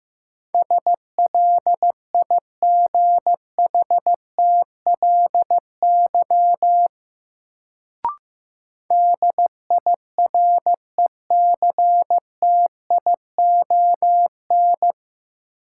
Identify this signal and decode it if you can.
Morse code, heard as a keyed tone, about 15 wpm, 700 Hz. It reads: SLIGHTLY DIRECTION
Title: Sentences from Top 1300 Words - New Word Drill Alt - 15wpm